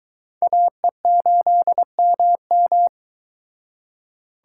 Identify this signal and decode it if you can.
Morse code, heard as a keyed tone, about 23 wpm, 700 Hz. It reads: AE8MM